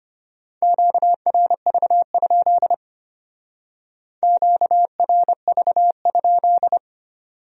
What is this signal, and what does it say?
Morse code, heard as a keyed tone, about 32 wpm, 700 Hz. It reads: QRV? QRV?